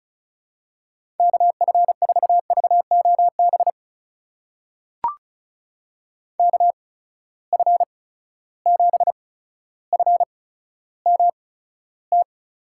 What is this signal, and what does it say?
Morse code, heard as a keyed tone, about 36 wpm, spaced as for 15 wpm, 700 Hz. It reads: KF4VOB KF7FMT
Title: USA Call Signs - 6 Characters - 8x Character Spacing - 35wpm